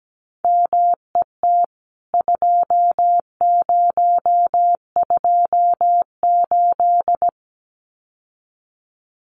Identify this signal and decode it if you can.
Morse code, heard as a keyed tone, about 17 wpm, 700 Hz. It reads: MET 2028
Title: CWT Exchange 17wpm